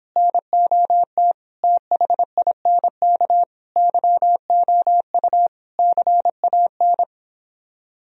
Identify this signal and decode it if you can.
Morse code, heard as a keyed tone, about 26 wpm, 700 Hz. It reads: NOT THINK YOU CAN